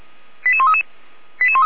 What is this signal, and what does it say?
Five-tone selective call (ZVEI), heard as quick runs of five tones, tones of about 70 ms. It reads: 88133
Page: SDR/ZVEI